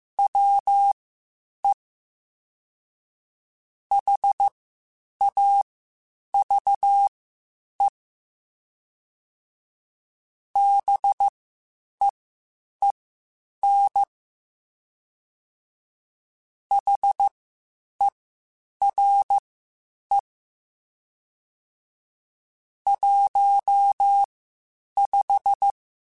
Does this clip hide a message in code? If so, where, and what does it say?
Yes: Morse code from 0.2 to 25.7 s, heard as a keyed tone, about 15 wpm, spaced as for 8 wpm, 800 Hz. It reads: WE HAVE BEEN HERE 15